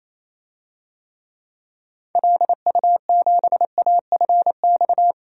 Morse code, heard as a keyed tone, about 28 wpm, 700 Hz. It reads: LU7AFX
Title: DXCC-100 Call Signs - 6 Characters - 28wpm